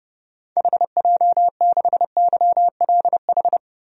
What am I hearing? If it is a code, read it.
Morse code, heard as a keyed tone, about 32 wpm, 700 Hz. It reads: HJ6YLH
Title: DXCC-100 Call Signs - 6 Characters - 30wpm